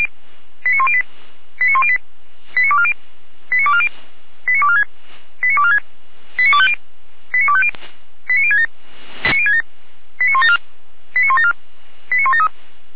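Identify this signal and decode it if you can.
Five-tone selective call (ZVEI), heard as quick runs of five tones, tones of about 70 ms. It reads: 89108 89108 89240 89240 89246 89246 89250 89250 89067 89067 89174 89174 89173
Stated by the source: Received on SDR, ZVEI five-tone call 89337, recorded by feuerwehr-orlamuende (fire brigade Orlamünde).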